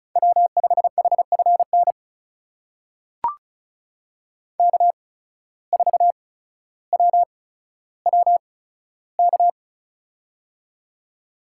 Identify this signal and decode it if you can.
Morse code, heard as a keyed tone, about 34 wpm, spaced as for 15 wpm, 700 Hz. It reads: W5HFN K4WWK
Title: USA Call Signs - 5 Characters - 8x Character Spacing - 35wpm